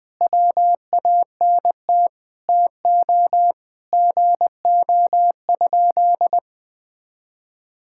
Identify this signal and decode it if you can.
Morse code, heard as a keyed tone, about 21 wpm, 700 Hz. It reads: WANT TO GO?
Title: Sentences with 2-Word Phrases - 20wpm